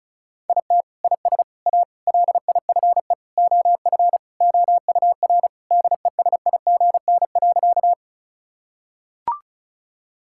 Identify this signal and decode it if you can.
Morse code, heard as a keyed tone, about 36 wpm, 700 Hz. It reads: IT IS A LIFE OF OUR DESIGN.
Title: Sentences from Top 1000 Words - Review Part 5 - 35wpm